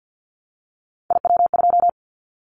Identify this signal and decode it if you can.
Morse code, heard as a keyed tone, about 50 wpm, 700 Hz. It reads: IW2
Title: Callsign Prefixes for Top 25 DXCC - 50wpm